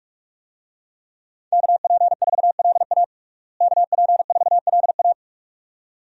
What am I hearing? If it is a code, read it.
Morse code, heard as a keyed tone, about 46 wpm, 700 Hz. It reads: KP4LA KP4LA
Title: USA Call Signs - 5 Characters - Repeated 2x - 45wpm